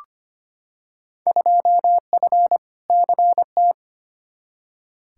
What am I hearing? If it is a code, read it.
Morse code, heard as a keyed tone, about 25 wpm, 700 Hz. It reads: 2F CT
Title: Field Day Exchanges Encore - 25wpm